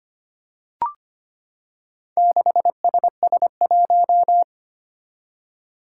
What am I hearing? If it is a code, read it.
Morse code, heard as a keyed tone, about 25 wpm, 700 Hz. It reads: 6SS1
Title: Four Challenging Letter-Number Combinations 25wpm